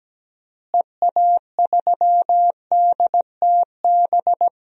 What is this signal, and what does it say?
Morse code, heard as a keyed tone, about 17 wpm, 700 Hz. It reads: EA3DTB
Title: DXCC-100 Call Signs - 6 Characters - Repeated 3x - 17wpm